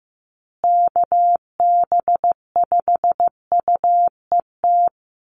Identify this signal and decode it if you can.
Morse code, heard as a keyed tone, about 15 wpm, 700 Hz. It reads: KB5UET
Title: USA Call Signs Repeated 3x 15wpm